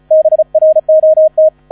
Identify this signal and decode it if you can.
Morse code, heard as a keyed tone, about 34 wpm, 630 Hz. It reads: BROT
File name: 35wpm